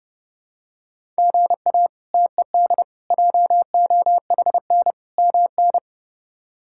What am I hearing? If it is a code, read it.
Morse code, heard as a keyed tone, about 31 wpm, 700 Hz. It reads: GA TED JOHN MN
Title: K1USN SST QSO Practice - 30wpm